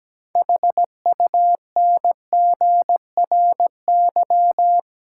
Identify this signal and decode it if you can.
Morse code, heard as a keyed tone, about 17 wpm, 700 Hz. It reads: HUNGRY
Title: Sentences from Top 1400 Words - New Word Drill Alt - 17wpm